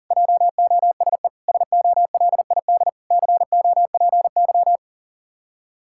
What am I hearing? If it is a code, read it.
Morse code, heard as a keyed tone, about 40 wpm, 700 Hz. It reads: JOSE SOLID COPY
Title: Practice QSO - Exchange 3 - More Info - 40wpm